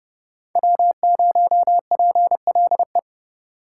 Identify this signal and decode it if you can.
Morse code, heard as a keyed tone, about 37 wpm, 700 Hz. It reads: W0PLE
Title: Speed Racing - USA 1x3 Call Signs 30wpm